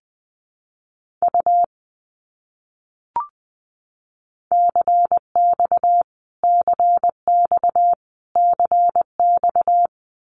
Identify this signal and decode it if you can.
Morse code, heard as a keyed tone, about 20 wpm, 700 Hz. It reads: U CX CX CX
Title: Abbreviations Repeated 3x - 20wpm